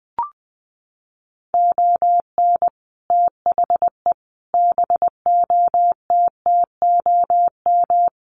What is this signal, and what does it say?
Morse code, heard as a keyed tone, about 20 wpm, 700 Hz. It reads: ON THE BOTTOM
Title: Sets of 5 Words Encore2 20wpm